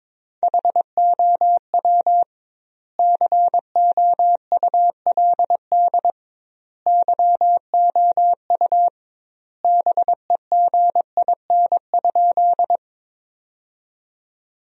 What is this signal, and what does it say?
Morse code, heard as a keyed tone, about 22 wpm, 700 Hz. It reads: HOW COULD YOU BEGIN?
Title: 2x Word-Spacing - Sentences from Top 300 Words - 22wpm